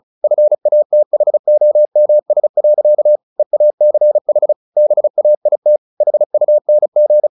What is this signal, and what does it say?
Morse code, heard as a keyed tone, about 34 wpm, 600 Hz. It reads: FATHOMS. EACH BAIT HUNG
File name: the old man and the sea / the old man and the sea 1 35wpm